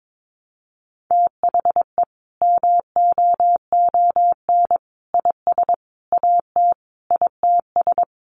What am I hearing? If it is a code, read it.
Morse code, heard as a keyed tone, about 22 wpm, 700 Hz. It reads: THE MOON IS AT ITS
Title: Sentences from Top 1300 Words 22wpm